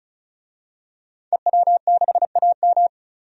Morse code, heard as a keyed tone, about 34 wpm, 700 Hz. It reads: EW6AM
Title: DXCC-100 Call Signs - 5 Characters - 35wpm